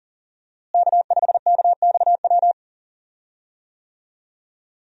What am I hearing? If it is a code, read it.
Morse code, heard as a keyed tone, about 40 wpm, 700 Hz. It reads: K5KXW
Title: USA Call Signs - 5 Characters - Repeated 3x - 40wpm